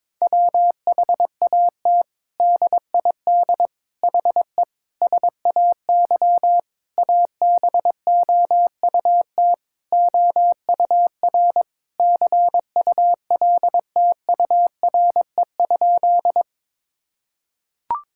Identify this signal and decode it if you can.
Morse code, heard as a keyed tone, about 22 wpm, 700 Hz. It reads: WHAT DID HE SAY ABOUT OUR CULTURE?
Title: Sentences from Top 1000 Words - Review Part 6 - 22wpm